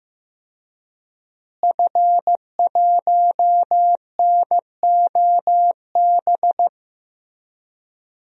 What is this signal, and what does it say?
Morse code, heard as a keyed tone, about 15 wpm, 700 Hz. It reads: F1NOB
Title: DXCC-100 Call Signs - 5 Characters - Repeated 3x - 15wpm